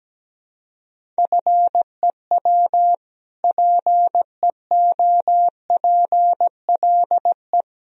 Morse code, heard as a keyed tone, about 17 wpm, 700 Hz. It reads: FEW PEOPLE
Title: Sentences from Top 1000 Words - Review Part 1 - 17wpm